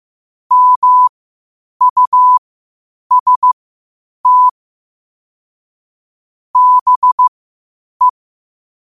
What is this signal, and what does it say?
Morse code, heard as a keyed tone, about 15 wpm, spaced as for 8 wpm, 1000 Hz. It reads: MUST BE
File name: 8wpm